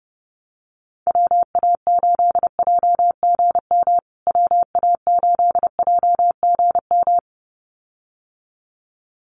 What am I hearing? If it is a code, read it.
Morse code, heard as a keyed tone, about 30 wpm, 700 Hz. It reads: WA8JGM WA8JGM
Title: USA Call Signs Repeated 2x 30wpm